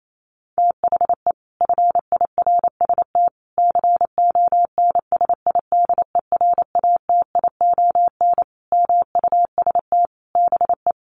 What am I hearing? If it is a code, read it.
Morse code, heard as a keyed tone, about 28 wpm, 700 Hz. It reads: THE FIRST CONSIDERATION MUST BE
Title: Sentences from Top 1500 Words 28wpm